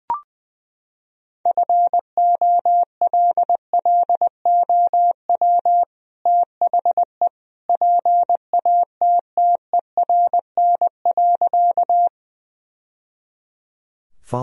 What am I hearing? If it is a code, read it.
Morse code, heard as a keyed tone, about 21 wpm, 700 Hz. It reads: FOLLOW THE PATTERN.
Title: Sentences from Top 1000 Words - Review Part 2 - 20wpm